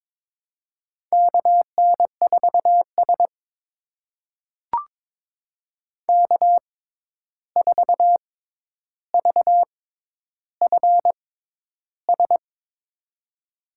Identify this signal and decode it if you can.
Morse code, heard as a keyed tone, about 22 wpm, spaced as for 11 wpm, 700 Hz. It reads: KN4S K4VFS
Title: USA Call Signs - 6x Character Spacing - 22wpm